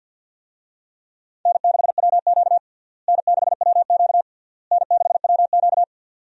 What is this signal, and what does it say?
Morse code, heard as a keyed tone, about 51 wpm, 700 Hz. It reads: N6WQ N6WQ N6WQ
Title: USA Call Signs - 4 Characters - Repeated 3x - 50wpm